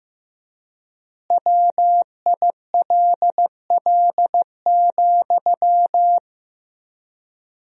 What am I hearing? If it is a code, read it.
Morse code, heard as a keyed tone, about 16 wpm, 700 Hz. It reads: WILL,
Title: 3x Word-Spacing - Sentences from Top 200 Words - 15wpm